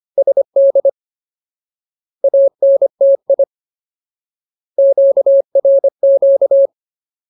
Morse code, heard as a keyed tone, about 25 wpm, 550 Hz. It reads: SD ANTI QRQ
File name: Ultimate Word List / 25WPM Ultimate